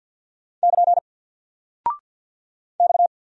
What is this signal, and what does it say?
Morse code, heard as a keyed tone, about 49 wpm, 700 Hz. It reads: <KN> X
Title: Single Character (Letter:Number:Punctuation:Prosign) Alt 50wpm